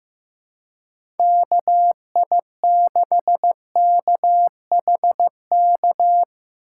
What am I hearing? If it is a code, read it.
Morse code, heard as a keyed tone, about 15 wpm, 700 Hz. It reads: KI6KHK
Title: USA Call Signs - 6 Characters - 2x Character Spacing - 15wpm